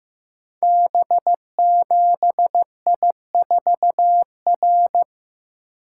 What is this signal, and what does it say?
Morse code, heard as a keyed tone, about 15 wpm, 700 Hz. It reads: B7I4R
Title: Code Groups - 5 Letter:Numbers - 15wpm